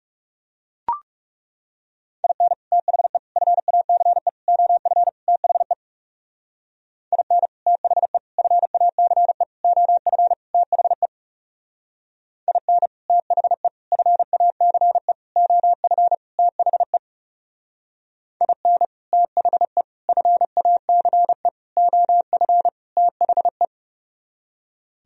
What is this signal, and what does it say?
Morse code, heard as a keyed tone, about 34 wpm, 700 Hz. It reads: IN THE FACE OF THE IN THE FACE OF THE IN THE FACE OF THE IN THE FACE OF THE
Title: Speed-Racing - Sets of 5 Words 30wpm